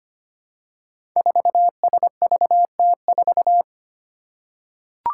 Morse code, heard as a keyed tone, about 25 wpm, 700 Hz. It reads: 4SVT4
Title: Five Challenging Letter-Number Combinations 25wpm